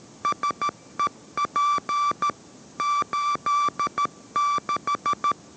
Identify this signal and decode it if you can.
Morse code, heard as a keyed tone, about 13 wpm, 1160 Hz. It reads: SEP 86